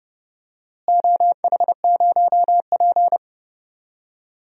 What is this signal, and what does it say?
Morse code, heard as a keyed tone, about 30 wpm, 700 Hz. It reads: OH0P
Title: DXCC-100 Call Signs - 4 Characters - Repeated 2x - 30wpm